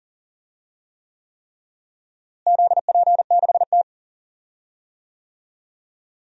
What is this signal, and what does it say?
Morse code, heard as a keyed tone, about 40 wpm, 700 Hz. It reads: ZP6T
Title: DXCC-100 Call Signs - 4 Characters - Repeated 3x - 40wpm